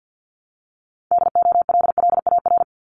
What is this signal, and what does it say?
Morse code, heard as a keyed tone, about 50 wpm, 700 Hz. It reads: DOLLAR
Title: Multi-letter Phonemes 50wpm